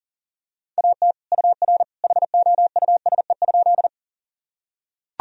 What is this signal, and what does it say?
Morse code, heard as a keyed tone, about 40 wpm, 700 Hz. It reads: AT UR HOUSE?
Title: Sentences from Top 200 Words and Abbreviations 40wpm